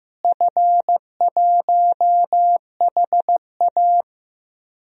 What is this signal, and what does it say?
Morse code, heard as a keyed tone, about 15 wpm, 700 Hz. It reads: F1HA
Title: DXCC-100 Call Signs - 4 Characters - Repeated 2x - 15wpm